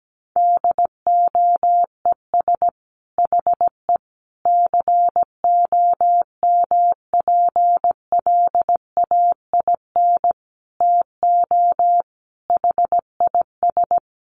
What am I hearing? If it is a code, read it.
Morse code, heard as a keyed tone, about 17 wpm, 700 Hz. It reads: DOES HE COMPLAIN TO HIS
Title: Sentences from Top 1300 Words 17wpm